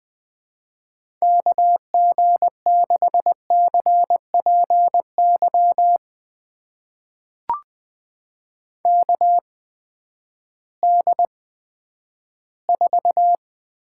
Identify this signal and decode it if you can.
Morse code, heard as a keyed tone, about 20 wpm, spaced as for 11 wpm, 700 Hz. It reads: KG6CPY KD4
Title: USA Call Signs - 6 Characters - 8x Character Spacing - 20wpm